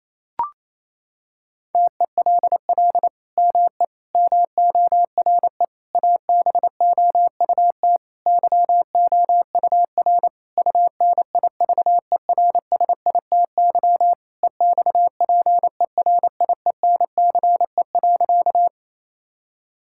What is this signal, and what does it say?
Morse code, heard as a keyed tone, about 28 wpm, 700 Hz. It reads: TELL ME MORE ABOUT YOUR UNIVERSITY EXPERIENCE.
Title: Sentences from Top 1000 Words - Review Part 5 - 28wpm